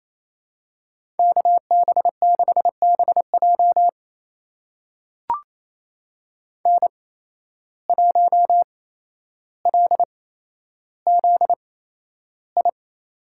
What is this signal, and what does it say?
Morse code, heard as a keyed tone, about 28 wpm, spaced as for 12 wpm, 700 Hz. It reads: KB6BJ N1LZI
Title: USA Call Signs - 5 Characters - 8x Character Spacing - 28wpm